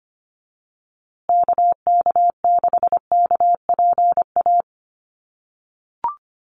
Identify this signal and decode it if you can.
Morse code, heard as a keyed tone, about 25 wpm, 700 Hz. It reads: KK6KPA
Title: USA Call Signs Repeated 3x 25wpm